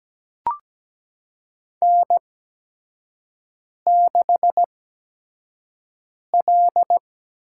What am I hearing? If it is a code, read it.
Morse code, heard as a keyed tone, about 17 wpm, spaced as for 5 wpm, 700 Hz. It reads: N6L
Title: USA Call Signs - 4 Characters - 8x Character Spacing - 17wpm